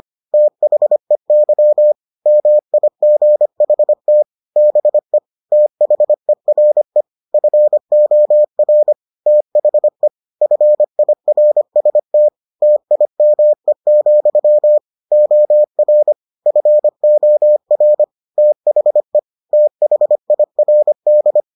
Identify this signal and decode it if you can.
Morse code, heard as a keyed tone, about 25 wpm, 600 Hz. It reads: THEY MIGHT BE THERE FOR THE FIRST TIME, OR FOR THE THIRD